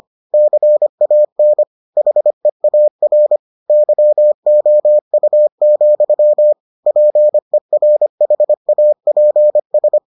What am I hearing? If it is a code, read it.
Morse code, heard as a keyed tone, about 25 wpm, 600 Hz. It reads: CAN HEAR YOU, PERHAPS